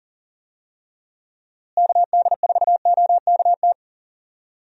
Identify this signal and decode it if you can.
Morse code, heard as a keyed tone, about 40 wpm, 700 Hz. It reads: KD4OKT
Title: USA Call Signs - 6 Characters - Repeated 3x - 40wpm